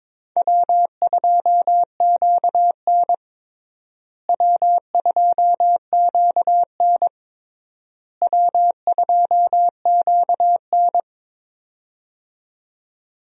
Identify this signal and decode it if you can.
Morse code, heard as a keyed tone, about 22 wpm, 700 Hz. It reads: W2QN W2QN W2QN
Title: USA Call Signs - 4 Characters - Repeated 3x - 22wpm